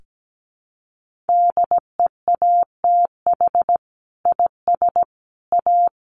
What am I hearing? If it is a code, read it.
Morse code, heard as a keyed tone, about 17 wpm, 700 Hz. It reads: DEATH IS A
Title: Sentences from Top 1500 Words 17wpm